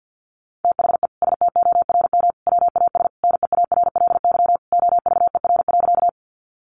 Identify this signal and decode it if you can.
Morse code, heard as a keyed tone, about 50 wpm, 700 Hz. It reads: THE STORM WAS NEARLY OVER.
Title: Sentences from Top 1500 Words 50wpm